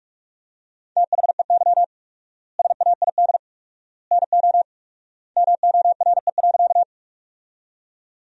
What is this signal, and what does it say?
Morse code, heard as a keyed tone, about 42 wpm, 700 Hz. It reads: THEY SAID NO MORE.
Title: 4x Word-Spacing - Sentences from Top 100 Words - 45wpm